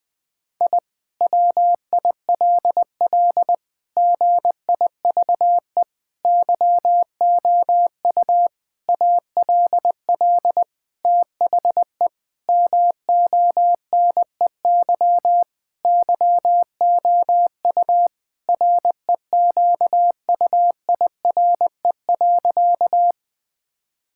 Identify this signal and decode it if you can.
Morse code, heard as a keyed tone, about 20 wpm, 700 Hz. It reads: I WILL GIVE YOU ALL THE MONEY YOU REQUIRE.
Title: Sentences from Top 1000 Words - Review Part 6 - 20wpm